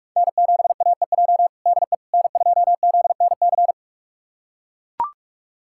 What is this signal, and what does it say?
Morse code, heard as a keyed tone, about 43 wpm, 700 Hz. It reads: N7AEJ DE N2ZNC
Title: USA Callsign DE Callsign 45wpm